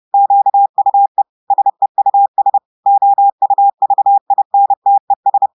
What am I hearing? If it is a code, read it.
Morse code, heard as a keyed tone, about 30 wpm, 800 Hz. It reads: QUE SEUS OUVINTES
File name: audios / portugais / portugais texte en clair 2 vitesse 30 mots minutes